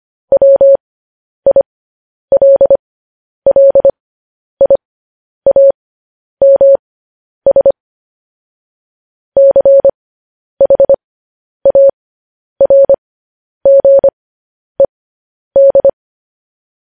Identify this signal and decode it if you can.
Morse code, heard as a keyed tone, about 25 wpm, spaced as for 10 wpm, 550 Hz. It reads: WILLIAMS CHARGED
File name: podcast / News headlines-1x-25x10